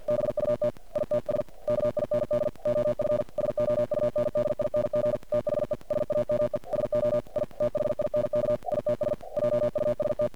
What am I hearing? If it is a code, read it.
Morse code, heard as a keyed tone, about 50 wpm, 600 Hz. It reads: BUT ITS KIND OF SOUNDING THE SAME SO I THINK ITS JUST
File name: archived / CRSnet